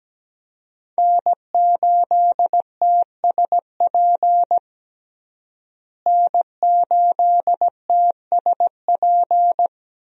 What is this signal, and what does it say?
Morse code, heard as a keyed tone, about 17 wpm, 700 Hz. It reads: N8TSP N8TSP
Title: USA Call Signs - 5 Characters - Repeated 2x - 17wpm